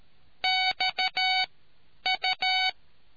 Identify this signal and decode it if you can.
Morse code, heard as a keyed tone, about 13 wpm, spaced as for 9 wpm, 750 Hz. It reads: XU